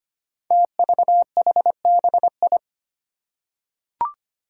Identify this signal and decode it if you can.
Morse code, heard as a keyed tone, about 25 wpm, 700 Hz. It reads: TVHBI
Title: Five Challenging Letter-Number Combinations 25wpm